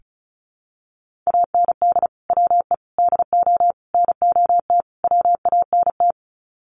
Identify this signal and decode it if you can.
Morse code, heard as a keyed tone, about 34 wpm, 700 Hz. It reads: AND WE DO NOT WANT
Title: Sets of 5 Words Encore2 35wpm